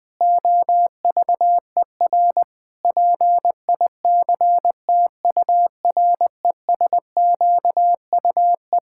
Morse code, heard as a keyed tone, about 21 wpm, 700 Hz. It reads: OVER PICTURESQUE
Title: Pangram Challenge v2 - 20wpm